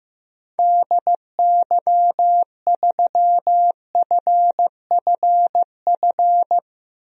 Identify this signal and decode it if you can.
Morse code, heard as a keyed tone, about 15 wpm, 700 Hz. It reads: DY3FFF
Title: DXCC-100 Call Signs - 6 Characters - 15wpm